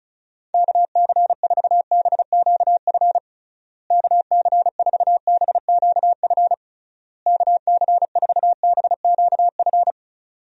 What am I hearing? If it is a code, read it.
Morse code, heard as a keyed tone, about 36 wpm, 700 Hz. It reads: KC4BQF KC4BQF KC4BQF
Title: USA Call Signs - 6 Characters - Repeated 3x - 35wpm